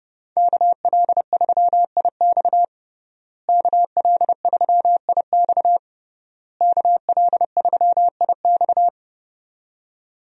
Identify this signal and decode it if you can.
Morse code, heard as a keyed tone, about 29 wpm, 700 Hz. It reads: KL3IX KL3IX KL3IX
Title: USA Call Signs - 5 Characters - Repeated 3x - 30wpm